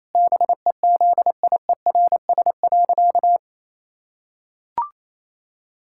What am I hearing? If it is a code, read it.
Morse code, heard as a keyed tone, about 27 wpm, 700 Hz. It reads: BEZIERS.
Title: Pangram Challenge v2 - 28wpm